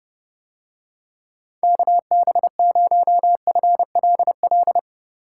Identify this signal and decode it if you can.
Morse code, heard as a keyed tone, about 30 wpm, 700 Hz. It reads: KB0FLL